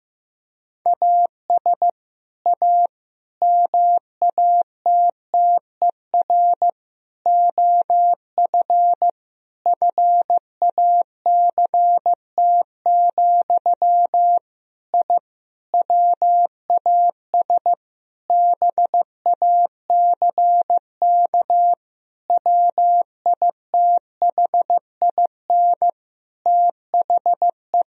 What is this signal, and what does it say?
Morse code, heard as a keyed tone, about 15 wpm, 700 Hz. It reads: AS A MATTER OF FACT, I WAS BACK WITHIN THE
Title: Sentences from Top 1000 Words - Review Part 4 - 15wpm